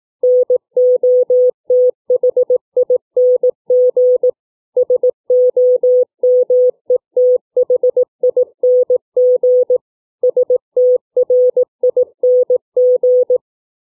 Morse code, heard as a keyed tone, about 18 wpm, 500 Hz. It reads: NOTHING SOMETHING STRING